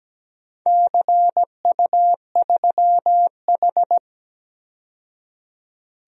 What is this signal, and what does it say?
Morse code, heard as a keyed tone, about 17 wpm, 700 Hz. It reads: CU3H
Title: DXCC-100 Call Signs - 4 Characters - 17wpm